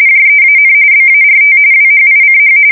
RTTY, about 45.45 baud. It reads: JEAN BART WHICH